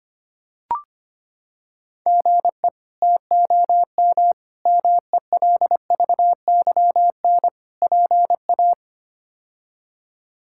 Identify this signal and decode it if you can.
Morse code, heard as a keyed tone, about 25 wpm, 700 Hz. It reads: GE TOM MELVYN PA
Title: K1USN SST QSO Practice - 25wpm